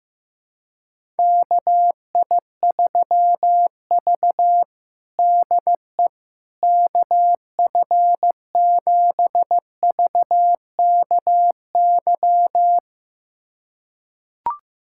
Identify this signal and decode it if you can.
Morse code, heard as a keyed tone, about 15 wpm, 700 Hz. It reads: KI3V DE KF7VKY
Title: USA Callsign DE Callsign 15wpm